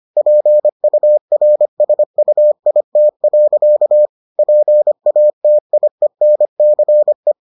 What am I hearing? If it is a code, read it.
Morse code, heard as a keyed tone, about 25 wpm, 600 Hz. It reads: PURSUIT. PATIENCE